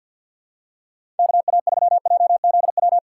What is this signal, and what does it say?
Morse code, heard as a keyed tone, about 50 wpm, 700 Hz. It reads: KA3JZW
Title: USA Call Signs - 6 Characters - 3x Character Spacing - 50wpm